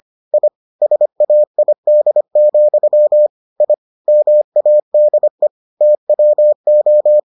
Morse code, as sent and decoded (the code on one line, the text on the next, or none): .. / ... .- .. -.. --..-- / .. / -- .- -.. / - .-- ---
I SAID, I MADE TWO